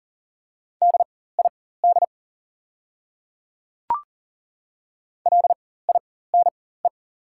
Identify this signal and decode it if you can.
Morse code, heard as a keyed tone, about 40 wpm, spaced as for 19 wpm, 700 Hz. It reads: DID LINE
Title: 4x Character Spacing - Top 300 Words - 40wpm